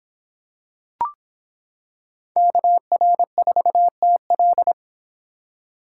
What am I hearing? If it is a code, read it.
Morse code, heard as a keyed tone, about 25 wpm, 700 Hz. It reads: KR4TL